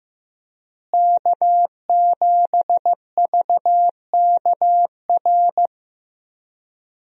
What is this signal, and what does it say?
Morse code, heard as a keyed tone, about 15 wpm, 700 Hz. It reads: K7VKR
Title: USA Call Signs - 5 Characters - Repeated 2x - 15wpm